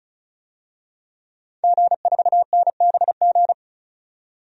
Morse code, heard as a keyed tone, about 36 wpm, 700 Hz. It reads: G4NBG
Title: DXCC-100 Call Signs - 5 Characters - 35wpm